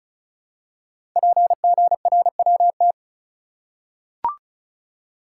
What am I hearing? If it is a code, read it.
Morse code, heard as a keyed tone, about 36 wpm, 700 Hz. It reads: PGRWT
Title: Code Groups - 5 Letters - 35wpm